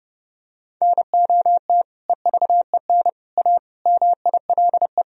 Morse code, heard as a keyed tone, about 32 wpm, 700 Hz. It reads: NOT EVEN A MILE
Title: Sentences from Top 1000 Words - Review Part 1 - 30wpm